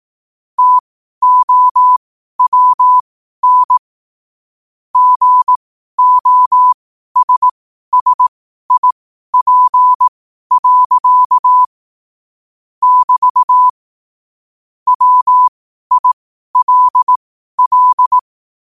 Morse code, heard as a keyed tone, about 18 wpm, spaced as for 12 wpm, 1000 Hz. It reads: TOWN GOSSIP. <BT> WILL
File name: qotdfiles / 12wpm